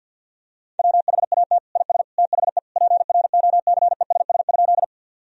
Morse code, heard as a keyed tone, about 52 wpm, 700 Hz. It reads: WHAT IS THE PROCESS?
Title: Sentences from Top 1000 Words - Review Part 3 - 50wpm